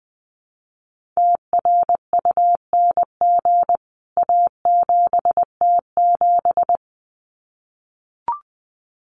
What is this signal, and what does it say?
Morse code, heard as a keyed tone, about 20 wpm, 700 Hz. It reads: TRUNG A7T7
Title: CWT Exchange 20wpm